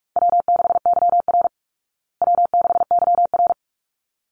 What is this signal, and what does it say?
Morse code, heard as a keyed tone, about 44 wpm, 700 Hz. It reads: W6YR W6YR
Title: Easy Call Signs Repeated 2x - 45wpm